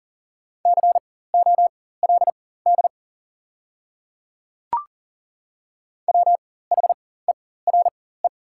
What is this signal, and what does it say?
Morse code, heard as a keyed tone, about 40 wpm, spaced as for 19 wpm, 690 Hz. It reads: COLD WHERE
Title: 4x Character Spacing - Top 500 Words - 40wpm